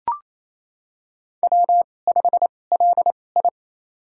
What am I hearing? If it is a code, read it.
Morse code, heard as a keyed tone, about 27 wpm, spaced as for 20 wpm, 700 Hz. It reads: W5LI
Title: USA Call Signs - 4 Characters - 2x Character Spacing - 28wpm